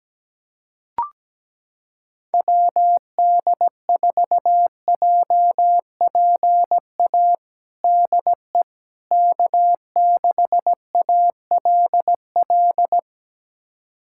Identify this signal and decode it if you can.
Morse code, heard as a keyed tone, about 17 wpm, 700 Hz. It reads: WD4JPA DE K6ALL
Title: USA Callsign DE Callsign 17wpm